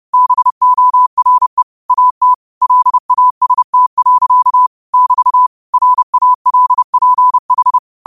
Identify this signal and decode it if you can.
Morse code, heard as a keyed tone, about 30 wpm, 1000 Hz. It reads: BORE AT LAST. <BT> RALPH
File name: qotdfiles / 30wpm